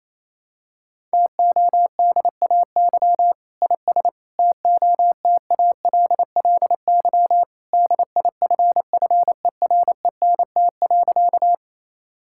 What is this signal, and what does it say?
Morse code, heard as a keyed tone, about 27 wpm, 700 Hz. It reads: TODAY IS TOTALLY DIFFERENT.
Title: Sentences from Top 1600 Words 28wpm